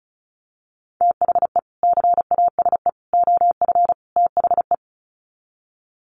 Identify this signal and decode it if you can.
Morse code, heard as a keyed tone, about 34 wpm, 700 Hz. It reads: THE CASE OF THE
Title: Sets of 4 Words 35wpm